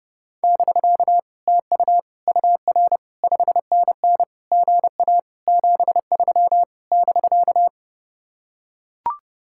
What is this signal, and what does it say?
Morse code, heard as a keyed tone, about 30 wpm, 700 Hz. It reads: <BK> TU UR 5NN GA 73 <BK>